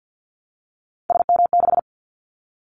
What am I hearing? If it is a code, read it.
Morse code, heard as a keyed tone, about 50 wpm, 700 Hz. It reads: SM6
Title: Callsign Prefixes for Top 25 DXCC - Repeated 3x - 50wpm